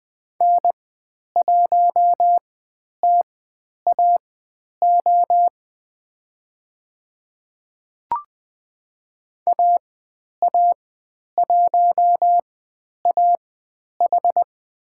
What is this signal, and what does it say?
Morse code, heard as a keyed tone, about 20 wpm, spaced as for 10 wpm, 700 Hz. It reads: N1TAO AA1AH
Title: 109 Lesson 18g - Letter m - Call Signs 20wpm